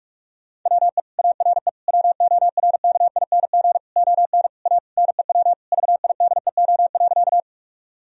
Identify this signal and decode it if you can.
Morse code, heard as a keyed tone, about 41 wpm, 700 Hz. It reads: WE ARE WORKING ON A NEW VIDEO.
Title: Sentences from Top 1000 Words - Review Part 4 - 45wpm